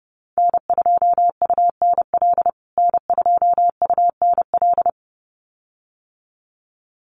Morse code, heard as a keyed tone, about 30 wpm, 700 Hz. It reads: N2UNL N2UNL
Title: USA Call Signs Repeated 2x 30wpm